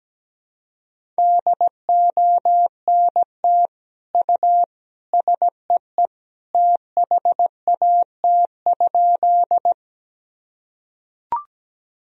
Morse code, heard as a keyed tone, about 17 wpm, 700 Hz. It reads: DONT U SEE THAT?
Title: Sentences from Top 300 Words and Abbreviations 17wpm